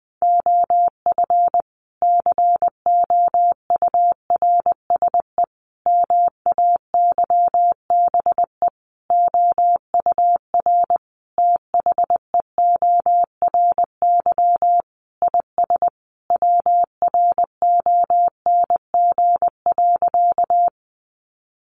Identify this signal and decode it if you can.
Morse code, heard as a keyed tone, about 20 wpm, 700 Hz. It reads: OF COURSE MAYBE OUR THEORY IS WRONG.
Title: A Sound Of Thunder 20wpm